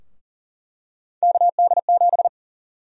Morse code, heard as a keyed tone, about 40 wpm, 700 Hz. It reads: KD7
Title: USA Call Sign Prefixes - Repeated 2x - 40wpm